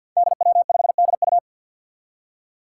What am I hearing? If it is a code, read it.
Morse code, heard as a keyed tone, about 47 wpm, 700 Hz. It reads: DW5DU
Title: Five Challenging Letter-Number Combinations 50wpm